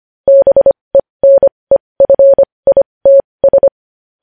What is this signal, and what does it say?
Morse code, heard as a keyed tone, about 25 wpm, spaced as for 20 wpm, 550 Hz. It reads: BENEFITS
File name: podcast / News headlines-1x-25x20